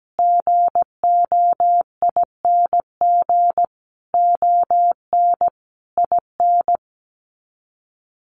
Morse code, heard as a keyed tone, about 17 wpm, 700 Hz. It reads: GOING ON IN
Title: Sets of 5 Words Encore2 17wpm